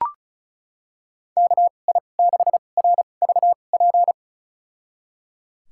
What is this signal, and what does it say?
Morse code, heard as a keyed tone, about 36 wpm, spaced as for 25 wpm, 700 Hz. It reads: KI6RVP
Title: USA Call Signs - 6 Characters - 2x Character Spacing - 35wpm